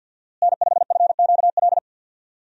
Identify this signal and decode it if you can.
Morse code, heard as a keyed tone, about 49 wpm, 700 Hz. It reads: N5FQL
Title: USA Call Signs - 5 Characters - Repeated 2x - 50wpm